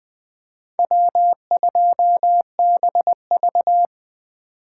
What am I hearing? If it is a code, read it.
Morse code, heard as a keyed tone, about 20 wpm, 700 Hz. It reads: W2BV
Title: Four Letter-Number Combinations - 20wpm